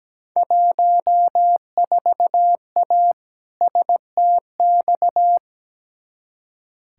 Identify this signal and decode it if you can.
Morse code, heard as a keyed tone, about 17 wpm, 700 Hz. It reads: 14A STX
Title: Field Day Exchanges Encore - 17wpm